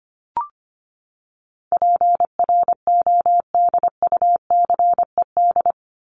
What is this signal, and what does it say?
Morse code, heard as a keyed tone, about 25 wpm, 700 Hz. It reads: PRODUCED
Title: ED Suffix - Top 100 Words with Base Words 25wpm